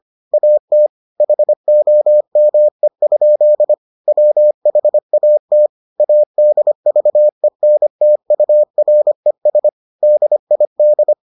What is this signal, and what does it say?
Morse code, heard as a keyed tone, about 25 wpm, 600 Hz. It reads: AT HOME? WHAT ADVENTURES DID